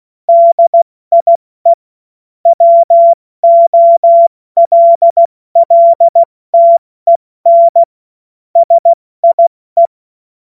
Morse code, heard as a keyed tone, about 16 wpm, spaced as for 14 wpm, 680 Hz. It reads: DIE WOLLTEN SIE
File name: Morsekurs / Witze / Witz 29. August 2023 70 F80